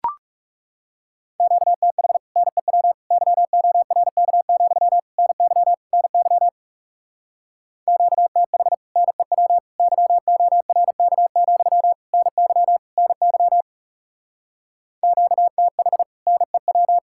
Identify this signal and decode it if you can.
Morse code, heard as a keyed tone, about 36 wpm, 700 Hz. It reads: QTH NEW YORK, NY NY QTH NEW YORK, NY NY QTH NEW
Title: Speed-Racing Common QSO Phrases - 30wpm